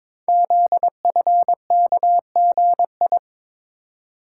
Morse code, heard as a keyed tone, about 22 wpm, 700 Hz. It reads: ZFKGI
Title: Code Groups - 5 Letters - 22wpm